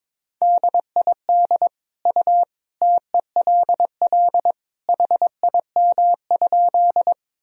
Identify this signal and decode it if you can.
Morse code, heard as a keyed tone, about 22 wpm, 700 Hz. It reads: DID U TELL HIM?
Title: Sentences from Top 200 Words and Abbreviations 22wpm